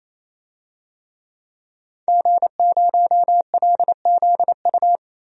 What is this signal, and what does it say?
Morse code, heard as a keyed tone, about 28 wpm, 700 Hz. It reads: G0LZU
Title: DXCC-100 Call Signs - 5 Characters - Repeated 3x - 28wpm